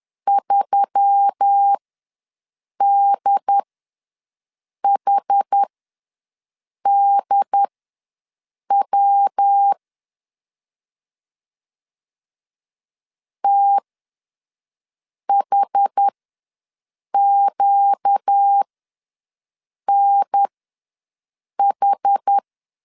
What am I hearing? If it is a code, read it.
Morse code, heard as a keyed tone, about 10 wpm, spaced as for 5 wpm, 800 Hz. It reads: 3DHDW THQNH
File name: LEZIONI / 5 LBVC / 5 L ws 10wpm 3636